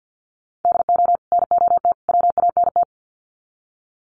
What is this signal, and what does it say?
Morse code, heard as a keyed tone, about 50 wpm, 700 Hz. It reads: DO NOT WANT
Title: Sets of 3 Words 50wpm